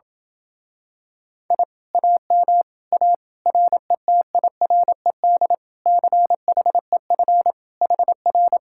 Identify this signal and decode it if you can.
Morse code, heard as a keyed tone, about 28 wpm, 700 Hz. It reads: I AM A RETIRED CHEF HR